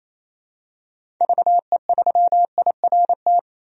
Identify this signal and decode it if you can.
Morse code, heard as a keyed tone, about 28 wpm, 700 Hz. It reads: VE3IRT
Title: DXCC-100 Call Signs - 6 Characters - 28wpm